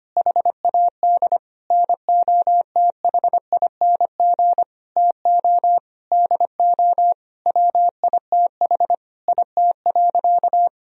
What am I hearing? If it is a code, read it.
Morse code, heard as a keyed tone, about 25 wpm, 700 Hz. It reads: HAD NOTHING TO DO WITH IT.
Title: Sentences from Top 1000 Words - Review Part 1 - 25wpm